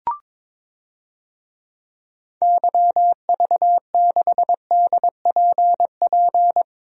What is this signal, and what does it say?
Morse code, heard as a keyed tone, about 22 wpm, 700 Hz. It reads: YV6DPP
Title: DXCC-100 Call Signs - 6 Characters - Repeated 2x - 22wpm